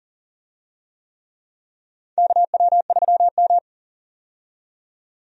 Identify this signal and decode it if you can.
Morse code, heard as a keyed tone, about 39 wpm, 700 Hz. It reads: KW3M
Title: USA Call Signs - 4 Characters - 40wpm